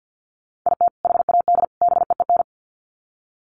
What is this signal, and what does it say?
Morse code, heard as a keyed tone, about 50 wpm, 700 Hz. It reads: IT HAD BEEN
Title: Sets of 3 Words 50wpm